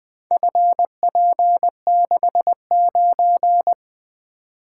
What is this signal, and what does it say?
Morse code, heard as a keyed tone, about 20 wpm, 700 Hz. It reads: FP69
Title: Four Letter-Number Combinations - 20wpm